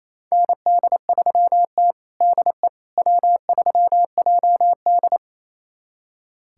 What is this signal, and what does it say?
Morse code, heard as a keyed tone, about 28 wpm, 700 Hz. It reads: ND3T DE W3JD
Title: USA Callsign DE Callsign - 4 Characters - 28wpm